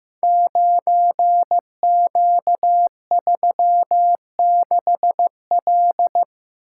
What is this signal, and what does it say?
Morse code, heard as a keyed tone, about 15 wpm, 700 Hz. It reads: 9Q36L
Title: Code Groups - 5 Letter:Numbers - 15wpm